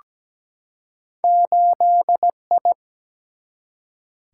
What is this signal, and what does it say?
Morse code, heard as a keyed tone, about 17 wpm, 700 Hz. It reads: 8I
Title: Single-Letter Single-Number 17wpm